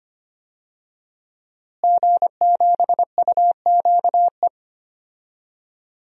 Morse code, heard as a keyed tone, about 25 wpm, 700 Hz. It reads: G7UQE
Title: DXCC-100 Call Signs - 5 Characters - Repeated 2x - 25wpm